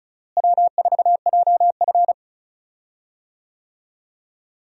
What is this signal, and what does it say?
Morse code, heard as a keyed tone, about 36 wpm, 700 Hz. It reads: W4JF
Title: USA Call Signs - 4 Characters - Repeated 3x - 35wpm